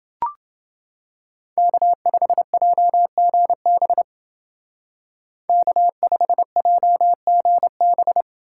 Morse code, heard as a keyed tone, about 28 wpm, 700 Hz. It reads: K5JGB K5JGB